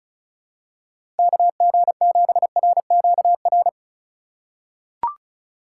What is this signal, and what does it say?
Morse code, heard as a keyed tone, about 36 wpm, 700 Hz. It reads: KG7RQR
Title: USA Call Signs - 6 Characters - 2x Character Spacing - 35wpm